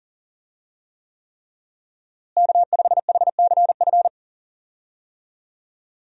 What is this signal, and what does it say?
Morse code, heard as a keyed tone, about 40 wpm, 700 Hz. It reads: K5HCF
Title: USA Call Signs - 5 Characters - Repeated 3x - 40wpm